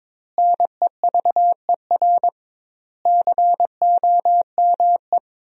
Morse code, heard as a keyed tone, about 22 wpm, 700 Hz. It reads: NEVER COME
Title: 2x Word-Spacing - Sentences from Top 300 Words - 22wpm